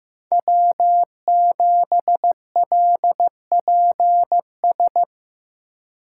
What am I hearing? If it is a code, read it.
Morse code, heard as a keyed tone, about 18 wpm, 700 Hz. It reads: W7LPS
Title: Speed Racing - USA 1x3 Call Signs 15wpm